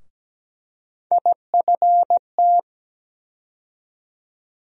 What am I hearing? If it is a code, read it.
Morse code, heard as a keyed tone, about 17 wpm, 700 Hz. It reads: IFT
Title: USA Call Sign Postfixes - 3 Letters 17wpm